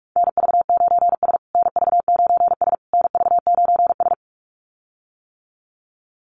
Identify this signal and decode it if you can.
Morse code, heard as a keyed tone, about 44 wpm, 700 Hz. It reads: NV9S NV9S NV9S
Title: USA Call Signs Repeated 3x 45wpm